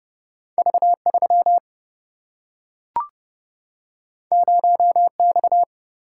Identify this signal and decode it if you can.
Morse code, heard as a keyed tone, about 30 wpm, 700 Hz. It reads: V 3 0 X
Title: Single-Letter Single-Number 30wpm